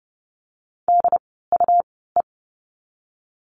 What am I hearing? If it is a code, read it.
Morse code, heard as a keyed tone, about 28 wpm, spaced as for 17 wpm, 700 Hz. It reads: DUE
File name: Lesson 17f - Letter u - Words 30wpm